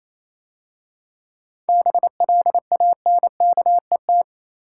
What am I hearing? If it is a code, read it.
Morse code, heard as a keyed tone, about 27 wpm, 700 Hz. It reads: BLANKET
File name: Easy Words - Speed-Racing with Hints - 22wpm